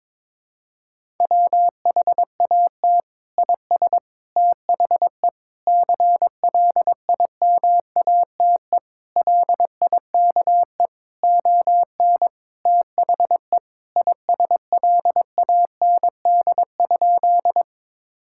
Morse code, heard as a keyed tone, about 22 wpm, 700 Hz. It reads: WHAT IS THE CLIMATE LIKE ON THE ISLAND?
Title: Sentences from Top 1600 Words 22wpm